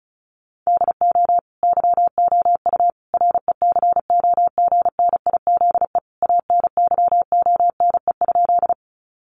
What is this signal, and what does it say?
Morse code, heard as a keyed tone, about 34 wpm, 700 Hz. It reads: DO YOU RECOGNIZE ANYONE?
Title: Sentences from Top 1400 Words 35wpm